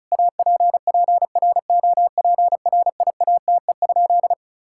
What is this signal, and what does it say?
Morse code, heard as a keyed tone, about 36 wpm, 700 Hz. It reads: APPROPRIATE?
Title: Sentences from Top 1000 Words - Review Part 5 - 35wpm